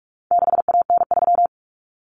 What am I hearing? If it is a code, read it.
Morse code, heard as a keyed tone, about 44 wpm, 700 Hz. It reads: 6AN3
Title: Common Cut Numbers - 4-Digit 45wpm